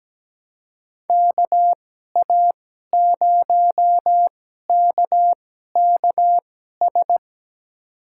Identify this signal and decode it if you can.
Morse code, heard as a keyed tone, about 17 wpm, spaced as for 12 wpm, 700 Hz. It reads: KA0KKS
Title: USA Call Signs - 6 Characters - 2x Character Spacing - 17wpm